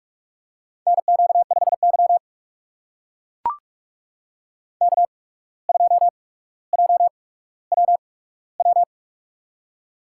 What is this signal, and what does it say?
Morse code, heard as a keyed tone, about 44 wpm, spaced as for 18 wpm, 700 Hz. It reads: NQ5Y K2JWW
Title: USA Call Signs - 8x Character Spacing - 45wpm